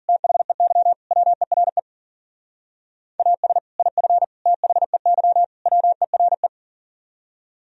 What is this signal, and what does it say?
Morse code, heard as a keyed tone, about 50 wpm, 700 Hz. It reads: THEY WERE AS IF THEY WERE